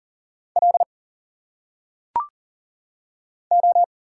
Morse code, heard as a keyed tone, about 40 wpm, 700 Hz. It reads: L O
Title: Single Letters - Phonetic Pronunciation 40wpm